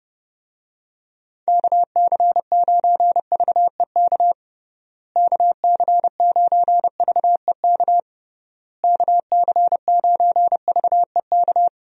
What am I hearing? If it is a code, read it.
Morse code, heard as a keyed tone, about 32 wpm, 700 Hz. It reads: KC9VEK KC9VEK KC9VEK
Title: USA Call Signs - 6 Characters - Repeated 3x - 30wpm